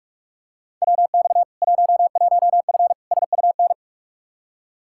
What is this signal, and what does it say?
Morse code, heard as a keyed tone, about 42 wpm, 700 Hz. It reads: WX 11F SUN
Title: Practice QSO - Exchange 2 - Info - 45wpm